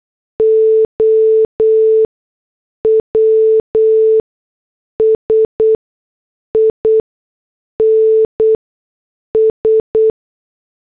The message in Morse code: OWS INS